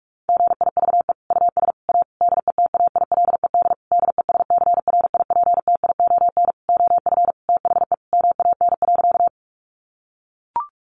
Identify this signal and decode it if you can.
Morse code, heard as a keyed tone, about 44 wpm, 700 Hz. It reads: GIVE US A DETAILED DESCRIPTION OF THE MAN.
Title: Sentences from Top 1500 Words 45wpm